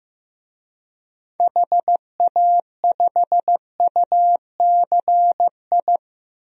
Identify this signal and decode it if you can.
Morse code, heard as a keyed tone, about 15 wpm, 700 Hz. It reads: HA5UCI
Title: DXCC-100 Call Signs - 6 Characters - 15wpm